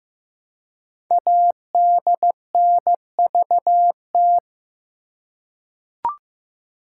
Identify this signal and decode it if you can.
Morse code, heard as a keyed tone, about 15 wpm, 700 Hz. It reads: ADNVT
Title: Code Groups - 5 Letters - 15wpm